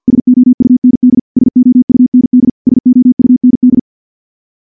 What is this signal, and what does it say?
Morse code, heard as a keyed tone, about 46 wpm, 250 Hz. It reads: SOUND SOUND SOUND
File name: morse / morse-training-50WPM